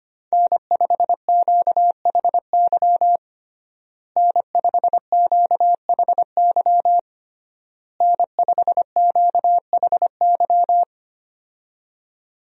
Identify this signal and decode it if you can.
Morse code, heard as a keyed tone, about 25 wpm, 700 Hz. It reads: N5QHY N5QHY N5QHY
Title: USA Call Signs - 5 Characters - Repeated 3x - 25wpm